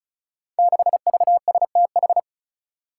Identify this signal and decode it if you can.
Morse code, heard as a keyed tone, about 36 wpm, 700 Hz. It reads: 6VSTH
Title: Five Challenging Letter-Number Combinations 35wpm